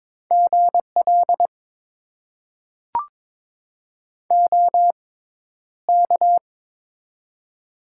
Morse code, heard as a keyed tone, about 21 wpm, spaced as for 11 wpm, 700 Hz. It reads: GL OK
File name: Top 90 QSO Elements - 6x Character Spacing - 22wpm